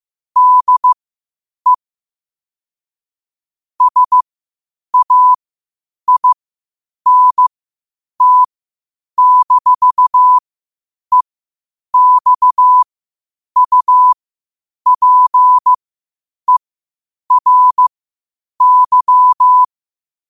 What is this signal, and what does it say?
Morse code, heard as a keyed tone, about 15 wpm, spaced as for 8 wpm, 1000 Hz. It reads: DE SAINT-EXUPERY
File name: qotdfiles / 8wpm